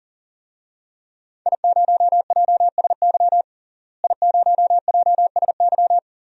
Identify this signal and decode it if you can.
Morse code, heard as a keyed tone, about 37 wpm, 700 Hz. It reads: I0JSY I0JSY
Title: DXCC-100 Call Signs - 5 Characters - Repeated 3x - 40wpm